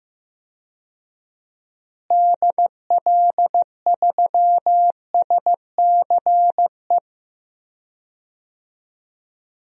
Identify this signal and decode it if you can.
Morse code, heard as a keyed tone, about 16 wpm, 700 Hz. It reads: DL3SCE
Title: DXCC-100 Call Signs - 6 Characters - Repeated 3x - 15wpm